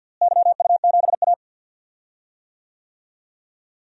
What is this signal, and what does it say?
Morse code, heard as a keyed tone, about 54 wpm, 700 Hz. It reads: YU7A
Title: DXCC-100 Call Signs - 4 Characters - Repeated 2x - 50wpm